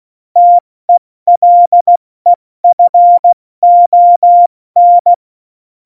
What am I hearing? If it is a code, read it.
Morse code, heard as a keyed tone, about 16 wpm, 700 Hz. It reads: TELEFON
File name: Morsekurs / Witze / Witz 05. Juni 2018, 70 F80